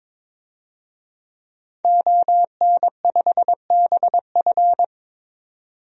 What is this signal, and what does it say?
Morse code, heard as a keyed tone, about 22 wpm, 700 Hz. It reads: ON5BF
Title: DXCC-100 Call Signs - 5 Characters - 22wpm